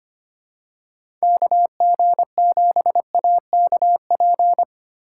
Morse code, heard as a keyed tone, about 25 wpm, 700 Hz. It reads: KG7AKP
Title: USA Call Signs - 6 Characters - 2x Character Spacing - 25wpm